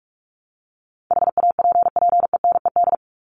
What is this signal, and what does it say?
Morse code, heard as a keyed tone, about 44 wpm, 700 Hz. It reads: HAPPENED
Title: ED Suffix - Top 100 Words with Base Words 45wpm